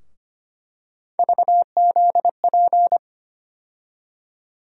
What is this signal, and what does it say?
Morse code, heard as a keyed tone, about 25 wpm, 700 Hz. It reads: VZP
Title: USA Call Sign Postfixes - 3 Letters 25wpm